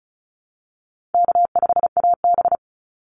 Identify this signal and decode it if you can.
Morse code, heard as a keyed tone, about 34 wpm, 700 Hz. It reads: K5AB
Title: Call Signs Repeated 3x - 35wpm